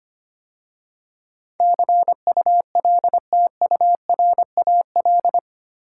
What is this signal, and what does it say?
Morse code, heard as a keyed tone, about 25 wpm, 700 Hz. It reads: CULTURAL
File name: 8 Letter Words - First 3 Letters Given - 25wpm